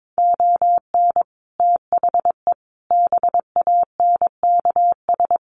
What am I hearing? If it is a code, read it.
Morse code, heard as a keyed tone, about 22 wpm, 700 Hz. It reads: ON THE BANKS
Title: Sets of 5 Words Encore2 22wpm